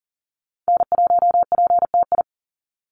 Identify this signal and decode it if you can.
Morse code, heard as a keyed tone, about 40 wpm, 700 Hz. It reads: N1PTI
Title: USA Call Signs 40wpm